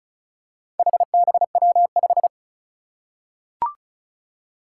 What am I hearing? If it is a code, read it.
Morse code, heard as a keyed tone, about 36 wpm, 700 Hz. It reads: HBW5
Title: Four Challenging Letter-Number Combinations 35wpm